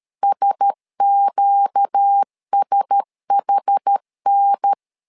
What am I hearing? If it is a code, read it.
Morse code, heard as a keyed tone, about 13 wpm, 800 Hz. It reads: SQSHN
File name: LEZIONI / 4 Q1S 12wpm 1802